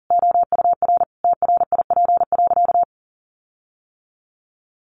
Morse code, heard as a keyed tone, about 40 wpm, 700 Hz. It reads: OUR TRIP.
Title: Sentences from Top 1300 Words 40wpm